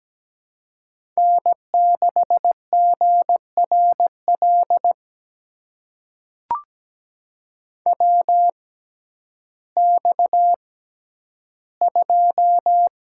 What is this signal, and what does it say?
Morse code, heard as a keyed tone, about 17 wpm, spaced as for 10 wpm, 700 Hz. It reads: N6GRL WX2
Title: USA Call Signs - 6x Character Spacing - 17wpm